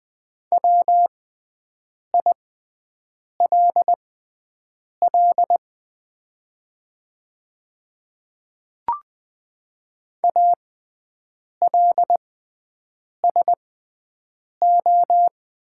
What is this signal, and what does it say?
Morse code, heard as a keyed tone, about 20 wpm, spaced as for 7 wpm, 690 Hz. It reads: WILL ALSO
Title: 6x Character Spacing - Top 300 Words - 20wpm